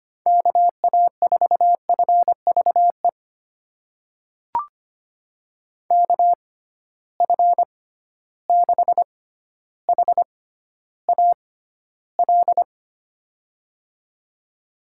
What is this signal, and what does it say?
Morse code, heard as a keyed tone, about 25 wpm, spaced as for 13 wpm, 700 Hz. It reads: KA4FVE KF6HAL
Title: USA Call Signs - 6 Characters - 6x Character Spacing - 25wpm